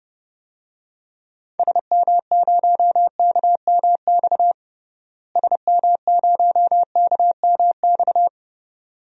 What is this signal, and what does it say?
Morse code, heard as a keyed tone, about 30 wpm, 700 Hz. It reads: SM0KMX SM0KMX
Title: DXCC-100 Call Signs - 6 Characters - Repeated 2x - 30wpm